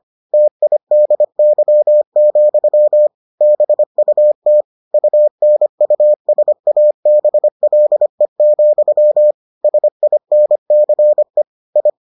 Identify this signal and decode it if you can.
Morse code, heard as a keyed tone, about 25 wpm, 600 Hz. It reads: TIDY, BUT UNUSABLE, SINCE I